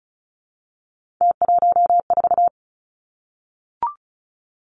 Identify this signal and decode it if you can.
Morse code, heard as a keyed tone, about 36 wpm, 700 Hz. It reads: T14
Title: Three Challenging Letter-Number Combinations 35wpm